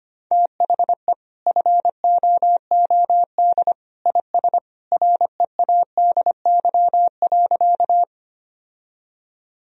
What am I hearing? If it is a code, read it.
Morse code, heard as a keyed tone, about 25 wpm, 700 Hz. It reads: THE FOOD IS READY.
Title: Sentences from Top 1000 Words - Review Part 1 - 25wpm